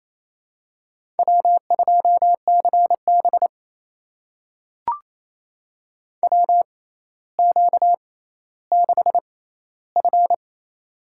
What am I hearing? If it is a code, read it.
Morse code, heard as a keyed tone, about 28 wpm, spaced as for 14 wpm, 700 Hz. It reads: W2CB WQ6F
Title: USA Call Signs - 4 Characters - 6x Character Spacing - 28wpm